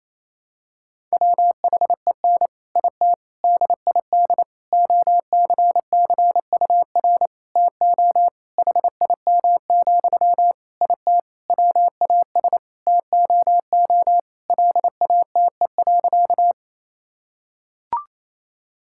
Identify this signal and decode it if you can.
Morse code, heard as a keyed tone, about 28 wpm, 700 Hz. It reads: WHEN IT DID OCCUR TO HIM, IT WAS TOO LATE.
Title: Sentences from Top 1000 Words - Review Part 6 - 28wpm